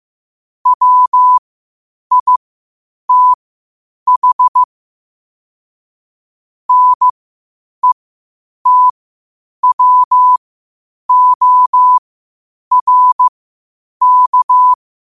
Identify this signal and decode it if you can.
Morse code, heard as a keyed tone, about 15 wpm, spaced as for 8 wpm, 1000 Hz. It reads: WITH NETWORK
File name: qotdfiles / 8wpm